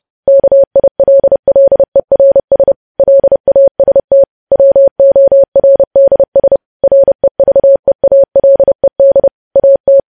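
Morse code, heard as a keyed tone, about 30 wpm, 550 Hz. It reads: KILLERS LAST WORDS REVEALED AT
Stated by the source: Morse code news headlines sent at 30, preceded by vvv vvv.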